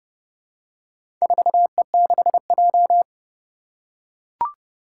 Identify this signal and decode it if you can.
Morse code, heard as a keyed tone, about 29 wpm, 700 Hz. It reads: 4E6J
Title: Four Challenging Letter-Number Combinations 30wpm